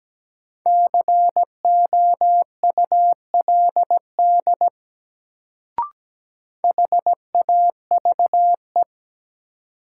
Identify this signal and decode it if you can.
Morse code, heard as a keyed tone, about 17 wpm, 700 Hz. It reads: COULD HAVE
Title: Sentences from Top 100 Words - New Word Drill - 17wpm